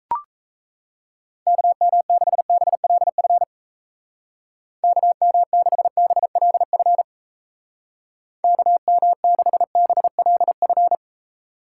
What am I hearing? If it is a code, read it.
Morse code, heard as a keyed tone, about 34 wpm, 700 Hz. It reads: KM6BLF KM6BLF KM6BLF
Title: Speed Racing - USA 2x3 Call Signs 28wpm